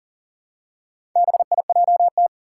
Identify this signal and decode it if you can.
Morse code, heard as a keyed tone, about 39 wpm, 700 Hz. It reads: BIJT
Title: Four Challenging Letter-Number Combinations 40wpm